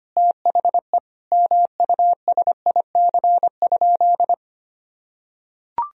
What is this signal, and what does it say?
Morse code, heard as a keyed tone, about 25 wpm, 700 Hz. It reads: THE MUSIC?
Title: Sentences from Top 300 Words and Abbreviations 25wpm